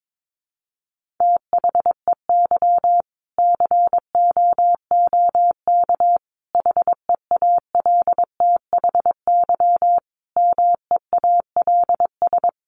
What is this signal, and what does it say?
Morse code, heard as a keyed tone, about 22 wpm, 700 Hz. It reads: THEY COOK HEALTHY MEALS
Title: Easy Phrases 22wpm